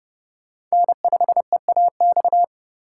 Morse code, heard as a keyed tone, about 37 wpm, 700 Hz. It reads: N5EAX
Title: Speed Racing - USA 1x3 Call Signs 30wpm